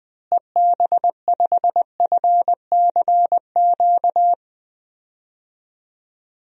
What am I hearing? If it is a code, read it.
Morse code, heard as a keyed tone, about 20 wpm, 700 Hz. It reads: EB5FCQ
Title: DXCC-100 Call Signs - 6 Characters - Repeated 2x - 20wpm